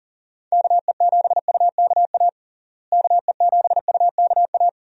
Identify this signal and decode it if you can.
Morse code, heard as a keyed tone, about 40 wpm, 700 Hz. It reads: KE7UKA KE7UKA
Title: USA Call Signs - 6 Characters - Repeated 2x - 40wpm